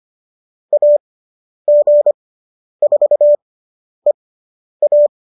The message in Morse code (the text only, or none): AG4EA